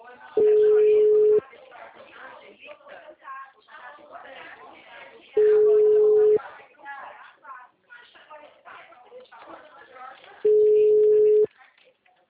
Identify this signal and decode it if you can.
Ringback tone (Europe, 425 Hz), with speech.
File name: out- -2030- - - .71551